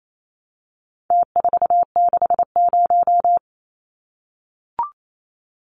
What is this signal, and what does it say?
Morse code, heard as a keyed tone, about 28 wpm, 700 Hz. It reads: T460
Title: Numbers - Contest Style - 4-Digit 28wpm